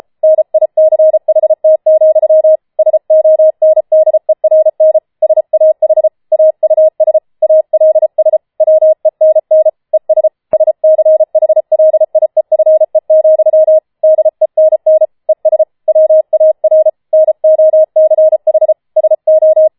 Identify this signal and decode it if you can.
Morse code, heard as a keyed tone, about 33 wpm, 630 Hz. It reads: NICHT, SONDERN SAH AUS ALS WENN ES SCHLIEFE, DENN ES WAR NOCH SO
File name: Schneewittchen Teil5 34wpm